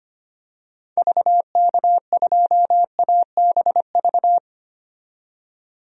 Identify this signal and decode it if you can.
Morse code, heard as a keyed tone, about 25 wpm, 700 Hz. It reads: VK2ABV
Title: DXCC-100 Call Signs - 6 Characters - 25wpm